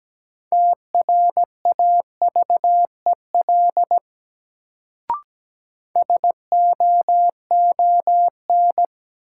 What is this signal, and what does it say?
Morse code, heard as a keyed tone, about 17 wpm, 700 Hz. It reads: TRAVEL SOON
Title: Sentences from Top 400 Words - New Word Drill - 17wpm